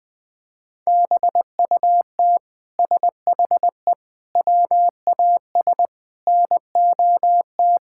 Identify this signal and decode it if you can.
Morse code, heard as a keyed tone, about 20 wpm, 700 Hz. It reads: BUT SHE WAS NOT
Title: Sentences from Top 1000 Words - Review Part 6 - 20wpm